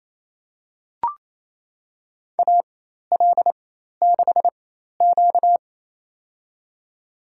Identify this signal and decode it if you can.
Morse code, heard as a keyed tone, about 28 wpm, spaced as for 13 wpm, 700 Hz. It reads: AL6Q
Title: USA Call Signs - 4 Characters - 4x Character Spacing - 28wpm